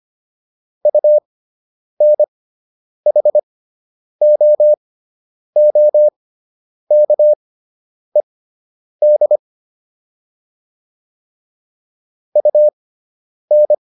Morse code, heard as a keyed tone, about 25 wpm, spaced as for 9 wpm, 600 Hz. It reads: UNHOOKED UN